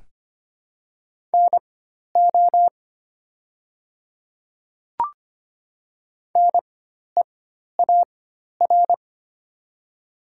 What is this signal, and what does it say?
Morse code, heard as a keyed tone, about 25 wpm, spaced as for 12 wpm, 700 Hz. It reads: NO NEAR
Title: 4x Character Spacing - Top 300 Words - 25wpm